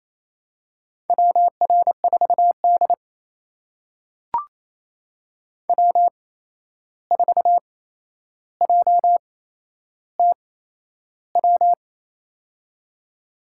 Morse code, heard as a keyed tone, about 28 wpm, spaced as for 11 wpm, 700 Hz. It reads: WR4D W4JTW
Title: USA Call Signs - 8x Character Spacing - 28wpm